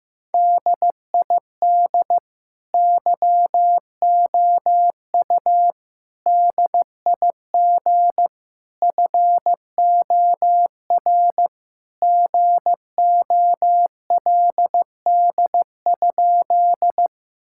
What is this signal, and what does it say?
Morse code, heard as a keyed tone, about 15 wpm, 700 Hz. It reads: DID YOU DIG FOR GOLD?
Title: Sentences from Top 1600 Words 15wpm